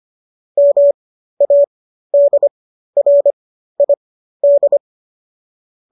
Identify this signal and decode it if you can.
Morse code, heard as a keyed tone, about 25 wpm, spaced as for 13 wpm, 570 Hz. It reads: MADRID